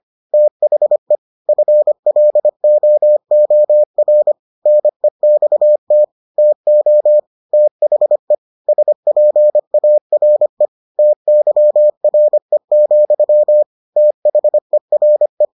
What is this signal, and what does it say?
Morse code, heard as a keyed tone, about 25 wpm, 600 Hz. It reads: THE FLOOR NEXT TO THE SPARE TYRE, THERE